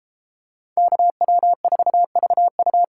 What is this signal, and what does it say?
Morse code, heard as a keyed tone, about 31 wpm, 700 Hz. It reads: KW4VU